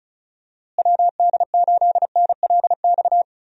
Morse code, heard as a keyed tone, about 36 wpm, 700 Hz. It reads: WD8NLX
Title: USA Call Signs - 6 Characters - Repeated 2x - 35wpm